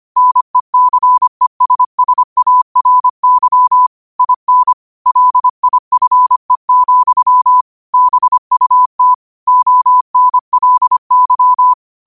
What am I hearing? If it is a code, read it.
Morse code, heard as a keyed tone, about 25 wpm, 1000 Hz. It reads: NECESSARY IN LIFE, BUT ONLY
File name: qotdfiles / 25wpm